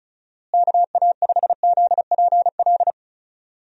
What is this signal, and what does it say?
Morse code, heard as a keyed tone, about 36 wpm, 700 Hz. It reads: KA5ZPL
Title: USA Call Signs - 6 Characters - Repeated 2x - 35wpm